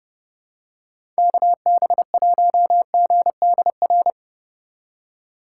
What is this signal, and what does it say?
Morse code, heard as a keyed tone, about 32 wpm, 700 Hz. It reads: KB1GDR
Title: USA Call Signs - 6 Characters - 3x Character Spacing - 30wpm